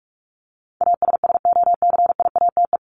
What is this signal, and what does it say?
Morse code, heard as a keyed tone, about 44 wpm, 700 Hz. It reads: ASSOCIATE
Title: Top 1751-2000 Words 45wpm